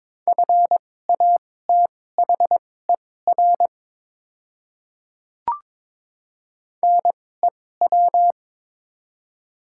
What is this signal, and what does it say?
Morse code, heard as a keyed tone, about 21 wpm, spaced as for 16 wpm, 700 Hz. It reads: FATHER NEW
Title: 2x Character Spacing - Top 200 Words - 22wpm